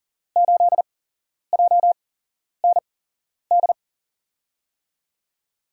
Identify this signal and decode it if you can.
Morse code, heard as a keyed tone, about 38 wpm, spaced as for 11 wpm, 700 Hz. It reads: 8JND
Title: USA Call Signs - 5 Characters - 8x Character Spacing - 40wpm